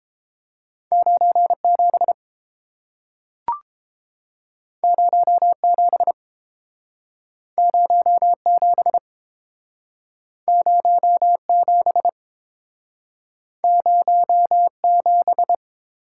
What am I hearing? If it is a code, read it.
Morse code, heard as a keyed tone, about 30 wpm, 700 Hz. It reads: 9 7 0 7 0 7 0 7 0 7